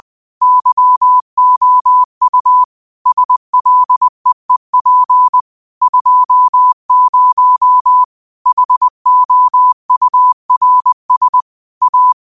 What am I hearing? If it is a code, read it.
Morse code, heard as a keyed tone, about 20 wpm, 1000 Hz. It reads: YOU SLEEP 20 HOURS A
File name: qotdfiles / 20wpm